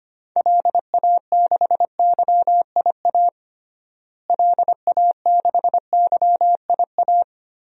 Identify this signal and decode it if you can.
Morse code, heard as a keyed tone, about 25 wpm, 700 Hz. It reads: LA6YIA LA6YIA
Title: DXCC-100 Call Signs - 6 Characters - Repeated 3x - 25wpm